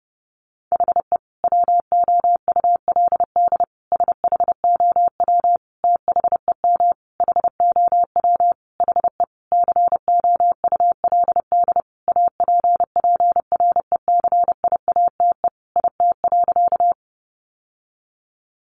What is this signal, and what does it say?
Morse code, heard as a keyed tone, about 30 wpm, 700 Hz. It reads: HE WOULD SHOW THEM HOW HE COULD APPRECIATE IT.
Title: Sentences from Top 1300 Words 30wpm